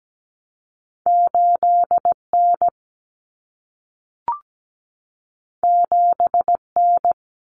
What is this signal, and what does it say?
Morse code, heard as a keyed tone, about 17 wpm, 700 Hz. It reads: 8N 7N
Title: Common Cut Numbers - 2-Digit 17wpm